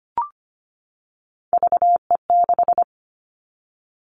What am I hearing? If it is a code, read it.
Morse code, heard as a keyed tone, about 25 wpm, 700 Hz. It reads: VE6
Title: Callsign Prefixes for Top 50 DXCC - 25wpm